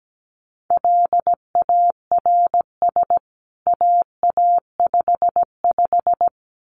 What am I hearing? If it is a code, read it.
Morse code, heard as a keyed tone, about 17 wpm, 700 Hz. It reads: LARS AA55
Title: CWT Exchange 17wpm